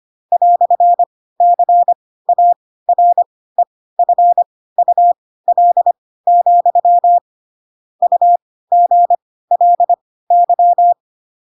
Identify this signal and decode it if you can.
Morse code, heard as a keyed tone, about 25 wpm, spaced as for 16 wpm, 700 Hz. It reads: "CAREFUL, UGLY
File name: Pinocchio / Pinocchio ch4 10 25 ik1hge